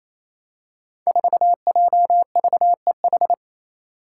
Morse code, heard as a keyed tone, about 28 wpm, 700 Hz. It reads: 4JVEH
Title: Five Challenging Letter-Number Combinations 28wpm